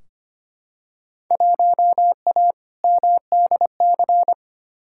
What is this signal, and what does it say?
Morse code, heard as a keyed tone, about 25 wpm, 700 Hz. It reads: 1A MDC
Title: Field Day Exchanges - 25wpm